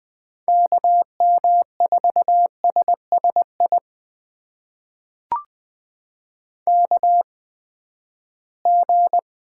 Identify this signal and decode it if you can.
Morse code, heard as a keyed tone, about 21 wpm, spaced as for 14 wpm, 700 Hz. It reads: KM4SSI KG
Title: USA Call Signs - 6 Characters - 8x Character Spacing - 20wpm